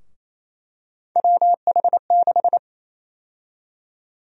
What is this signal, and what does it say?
Morse code, heard as a keyed tone, about 28 wpm, 700 Hz. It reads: WH6
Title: USA Call Sign Prefixes - Repeated 2x - 28wpm